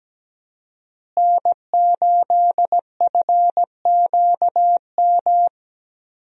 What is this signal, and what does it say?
Morse code, heard as a keyed tone, about 17 wpm, 700 Hz. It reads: N8FQM
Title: USA Call Signs - 5 Characters - Repeated 2x - 17wpm